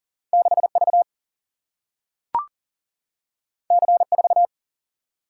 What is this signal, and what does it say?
Morse code, heard as a keyed tone, about 39 wpm, 700 Hz. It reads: 6V C4
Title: Single-Letter Single-Number 40wpm